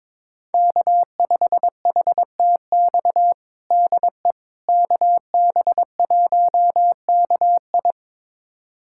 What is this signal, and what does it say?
Morse code, heard as a keyed tone, about 22 wpm, 700 Hz. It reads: K5HTX DE KB1KI
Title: USA Callsign DE Callsign - 5 Characters - 22wpm